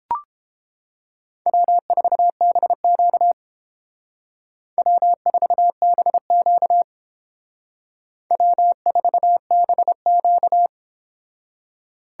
Morse code, heard as a keyed tone, about 28 wpm, 700 Hz. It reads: W4BQ W4BQ W4BQ